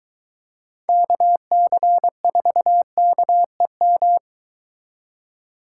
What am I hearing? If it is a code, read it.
Morse code, heard as a keyed tone, about 23 wpm, 700 Hz. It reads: KC4KEM